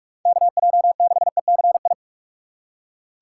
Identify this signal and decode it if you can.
Morse code, heard as a keyed tone, about 45 wpm, 700 Hz. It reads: KJ6ECI
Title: USA Call Signs - 6 Characters - 3x Character Spacing - 45wpm